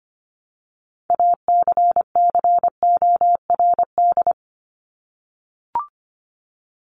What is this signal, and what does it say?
Morse code, heard as a keyed tone, about 25 wpm, 700 Hz. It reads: ACCORD
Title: ING Suffix - Top 100 Words with Base Words 25wpm